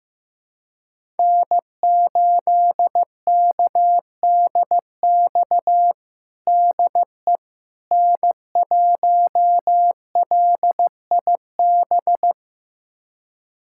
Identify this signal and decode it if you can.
Morse code, heard as a keyed tone, about 15 wpm, 700 Hz. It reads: N8KDX DE N1LIB
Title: USA Callsign DE Callsign - 5 Characters - 15wpm